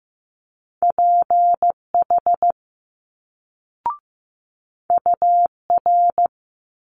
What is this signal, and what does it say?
Morse code, heard as a keyed tone, about 15 wpm, 700 Hz. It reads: PH UR
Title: Multi-letter Phonemes Quick Fire 15wpm